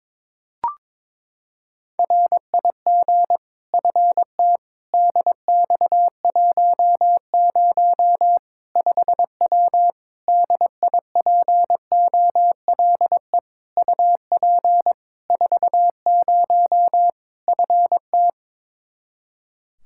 Morse code, heard as a keyed tone, about 22 wpm, 700 Hz. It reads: RIG FTDX10 5W DIPOLE UP 40 FT
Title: Practice QSO - Exchange 2 - Info - 22wpm